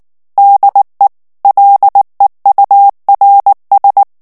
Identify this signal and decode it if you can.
Morse code, heard as a keyed tone, about 19 wpm, 800 Hz. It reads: DE LEURS
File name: FRANCAIS TEXTE CLAIR 1 VITESSE 18 MOTS MINUTES